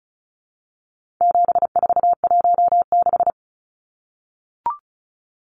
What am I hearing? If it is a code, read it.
Morse code, heard as a keyed tone, about 34 wpm, 700 Hz. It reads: 7416
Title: Numbers - 4-Digit 35wpm